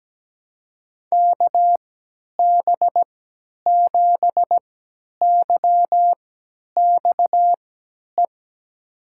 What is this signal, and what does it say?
Morse code, heard as a keyed tone, about 17 wpm, spaced as for 10 wpm, 700 Hz. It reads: KB7YXE
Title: USA Call Signs - 6 Characters - 3x Character Spacing - 17wpm